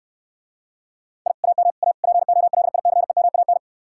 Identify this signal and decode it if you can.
Morse code, heard as a keyed tone, about 69 wpm, 700 Hz. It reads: I AM A POLICEMAN